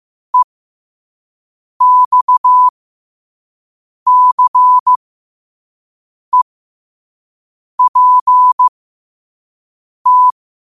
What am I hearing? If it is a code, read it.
Morse code, heard as a keyed tone, about 15 wpm, spaced as for 5 wpm, 1000 Hz. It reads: EXCEPT
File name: qotdfiles / 5wpm